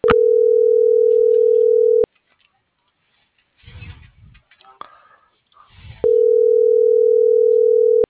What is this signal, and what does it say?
Ringback tone (North America), 440 + 480 Hz.